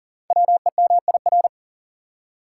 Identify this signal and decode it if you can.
Morse code, heard as a keyed tone, about 42 wpm, 690 Hz. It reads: WEMIR
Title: Code Groups - 5 Letters - 40wpm